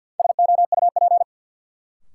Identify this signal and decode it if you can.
Morse code, heard as a keyed tone, about 53 wpm, 700 Hz. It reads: SOUP
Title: Sentences from Top 1600 Words - New Word Drill Alt - 50wpm